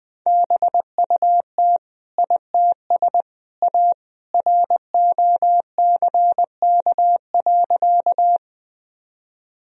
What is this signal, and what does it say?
Morse code, heard as a keyed tone, about 20 wpm, 700 Hz. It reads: BUT ITS A ROCK.
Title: Sentences from Top 1000 Words - Review Part 2 - 20wpm